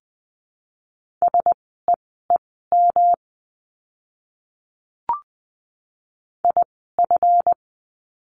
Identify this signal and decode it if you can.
Morse code, heard as a keyed tone, about 20 wpm, spaced as for 15 wpm, 700 Hz. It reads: SEEM IF
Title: 237 Lesson 38f - Letter q - Words 20wpm